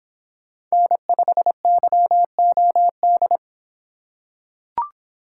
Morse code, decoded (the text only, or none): N5YOD